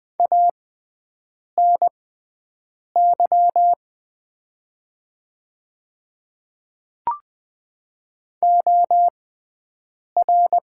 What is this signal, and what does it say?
Morse code, heard as a keyed tone, about 20 wpm, spaced as for 7 wpm, 700 Hz. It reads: ANY OR
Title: 6x Character Spacing - Top 200 Words - 20wpm